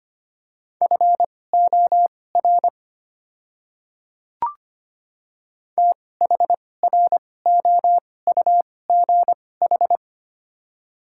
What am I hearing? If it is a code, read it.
Morse code, heard as a keyed tone, about 25 wpm, spaced as for 18 wpm, 690 Hz. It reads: FOR THROUGH
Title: 2x Character Spacing - Top 200 Words - 25wpm